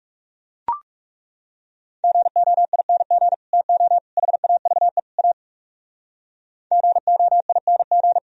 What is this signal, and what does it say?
Morse code, heard as a keyed tone, about 38 wpm, 700 Hz. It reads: GOING TO HAVE A GOING